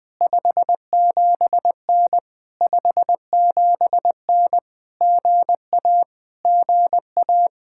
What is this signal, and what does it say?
Morse code, heard as a keyed tone, about 20 wpm, 700 Hz. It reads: 57N 57N GA GA
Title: POTA QSO Practice - 20wpm